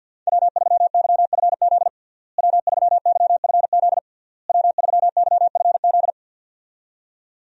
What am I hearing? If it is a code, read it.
Morse code, heard as a keyed tone, about 52 wpm, 700 Hz. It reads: W3YFZ W3YFZ W3YFZ
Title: USA Call Signs - 5 Characters - Repeated 3x - 50wpm